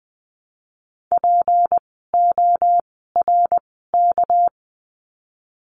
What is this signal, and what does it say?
Morse code, heard as a keyed tone, about 20 wpm, spaced as for 14 wpm, 700 Hz. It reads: PORK
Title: 199 Lesson 32f - Letter k - Words 20wpm